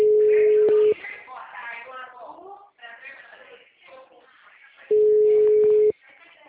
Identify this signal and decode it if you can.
Ringback tone (Europe, 425 Hz), with speech.